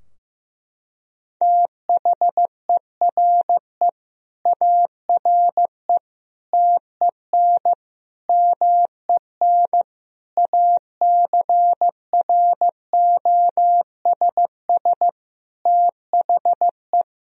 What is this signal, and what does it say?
Morse code, heard as a keyed tone, about 15 wpm, 700 Hz. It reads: THERE ARE TEN MEN ACROSS THE
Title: Sentences from Top 1000 Words - Review Part 2 - 15wpm